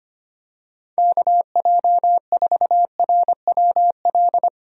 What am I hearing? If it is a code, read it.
Morse code, heard as a keyed tone, about 25 wpm, 700 Hz. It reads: KJ4RWL
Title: USA Call Signs - 6 Characters - Repeated 3x - 25wpm